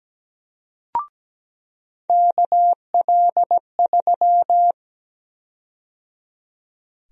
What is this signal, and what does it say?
Morse code, heard as a keyed tone, about 17 wpm, 700 Hz. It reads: KL3
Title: USA Call Sign Prefixes - 17wpm